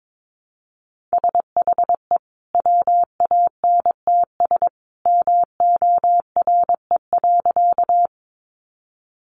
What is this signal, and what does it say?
Morse code, heard as a keyed tone, about 22 wpm, 700 Hz. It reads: SHE WANTS MORE.
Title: Easy Phrases 22wpm